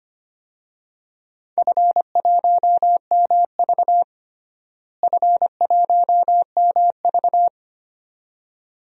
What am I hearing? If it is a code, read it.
Morse code, heard as a keyed tone, about 25 wpm, 700 Hz. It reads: F1MV F1MV
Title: DXCC-100 Call Signs - 4 Characters - Repeated 2x - 25wpm